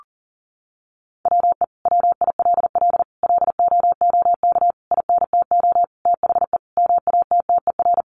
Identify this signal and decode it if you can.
Morse code, heard as a keyed tone, about 40 wpm, 700 Hz. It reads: WE WILL LOOK INTO THE MATTER
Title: Sentences from Top 1400 Words 40wpm